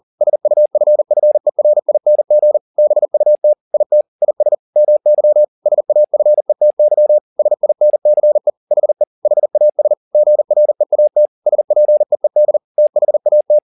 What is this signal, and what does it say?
Morse code, heard as a keyed tone, about 40 wpm, 600 Hz. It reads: SUFFERING BUT IT IS MY SAFETY SINCE HE HAS GREAT SPEED THAT
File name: the old man and the sea 2 40wpm